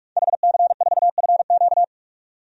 Morse code, heard as a keyed tone, about 47 wpm, 700 Hz. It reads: HC4FQ
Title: DXCC-100 Call Signs - 5 Characters - 45wpm